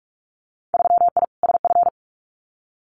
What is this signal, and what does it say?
Morse code, heard as a keyed tone, about 44 wpm, 700 Hz. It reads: 3I SF
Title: Winter Field Day Exchanges 45wpm